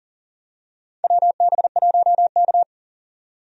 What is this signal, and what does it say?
Morse code, heard as a keyed tone, about 38 wpm, 700 Hz. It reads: WB1K
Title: USA Call Signs - 4 Characters - 40wpm